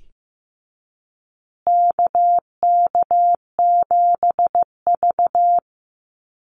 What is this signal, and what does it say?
Morse code, heard as a keyed tone, about 15 wpm, 700 Hz. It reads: KK7V
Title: USA Call Signs Repeated 2x 15wpm